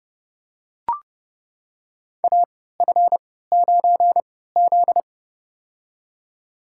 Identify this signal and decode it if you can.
Morse code, heard as a keyed tone, about 30 wpm, spaced as for 17 wpm, 700 Hz. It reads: AF9Z
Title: USA Call Signs - 4 Characters - 3x Character Spacing - 30wpm